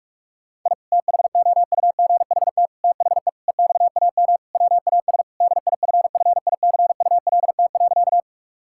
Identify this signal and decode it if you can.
Morse code, heard as a keyed tone, about 46 wpm, 700 Hz. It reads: I THOUGHT THE EXAM WAS DIFFICULT.
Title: Sentences from Top 1000 Words - Review Part 5 - 45wpm